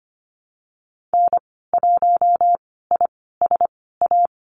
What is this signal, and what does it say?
Morse code, heard as a keyed tone, about 25 wpm, spaced as for 16 wpm, 700 Hz. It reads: N1ISA
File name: Lesson 15g - Number 5 - Call Signs 25wpm